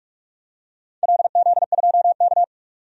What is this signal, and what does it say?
Morse code, heard as a keyed tone, about 46 wpm, 700 Hz. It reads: LZ2K
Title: DXCC-100 Call Signs - 4 Characters - 45wpm